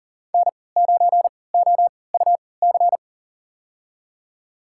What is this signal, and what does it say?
Morse code, heard as a keyed tone, about 40 wpm, spaced as for 23 wpm, 700 Hz. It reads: N9OUC
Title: USA Call Signs - 3x Character Spacing - 40wpm